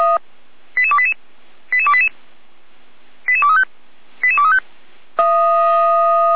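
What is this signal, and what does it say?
Five-tone selective call (ZVEI), heard as quick runs of five tones, tones of about 70 ms. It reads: 88288 88288 88235 88235
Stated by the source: ZVEI five-tone selective call received on SDR.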